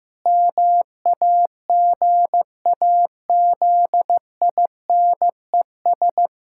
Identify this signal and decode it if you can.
Morse code, heard as a keyed tone, about 16 wpm, 700 Hz. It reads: MAGAZINES
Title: Pangram Challenge v2 - 15wpm